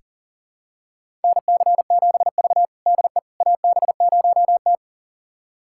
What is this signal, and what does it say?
Morse code, heard as a keyed tone, about 40 wpm, 700 Hz. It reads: NC7V DE AB0T
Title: USA Callsign DE Callsign - 4 Characters - 40wpm